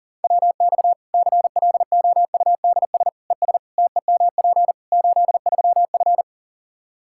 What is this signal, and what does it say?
Morse code, heard as a keyed tone, about 40 wpm, 700 Hz. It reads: WX CLOUDS ES TEMP 83F
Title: K1USN SST QSO Practice 40wpm